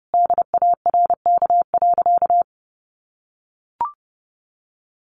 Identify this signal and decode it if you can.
Morse code, heard as a keyed tone, about 30 wpm, 700 Hz. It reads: DARK.
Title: Sentences from Top 700 Words 30wpm